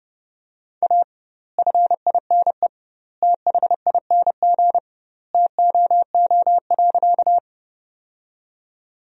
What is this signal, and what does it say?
Morse code, heard as a keyed tone, about 30 wpm, 700 Hz. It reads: A FINE THING TOO.
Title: 2x Word-Spacing - Sentences from Top 500 Words - 30wpm